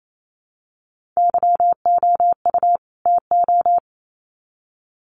Morse code, heard as a keyed tone, about 28 wpm, 700 Hz. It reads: YOU TO
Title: Sets of 2 Words 28wpm